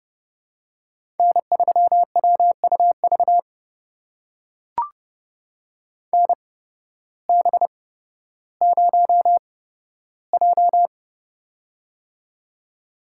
Morse code, heard as a keyed tone, about 30 wpm, spaced as for 14 wpm, 700 Hz. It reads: N3WUV NB0J
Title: USA Call Signs - 8x Character Spacing - 30wpm